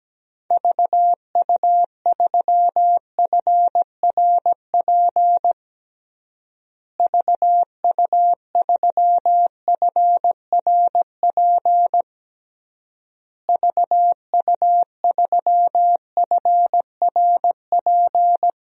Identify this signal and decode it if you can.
Morse code, heard as a keyed tone, about 17 wpm, 700 Hz. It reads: VU3FRP VU3FRP VU3FRP
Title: DXCC-100 Call Signs - 6 Characters - Repeated 3x - 17wpm